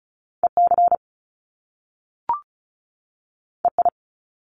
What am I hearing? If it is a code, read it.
Morse code, heard as a keyed tone, about 36 wpm, 700 Hz. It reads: EC EI
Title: Top Two Letter Combinations in Words 35wpm